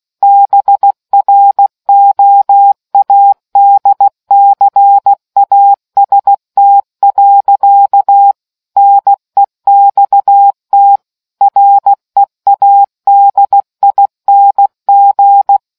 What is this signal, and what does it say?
Morse code, heard as a keyed tone, about 16 wpm, 800 Hz. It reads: BROADCAST. NEXT READING